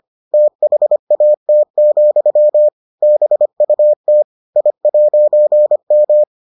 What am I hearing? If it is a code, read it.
Morse code, heard as a keyed tone, about 25 wpm, 600 Hz. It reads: THAT, BUT I'M